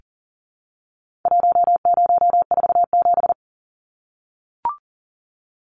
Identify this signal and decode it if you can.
Morse code, heard as a keyed tone, about 40 wpm, 700 Hz. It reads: 1047
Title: Numbers - Contest Style - 4-Digit 40wpm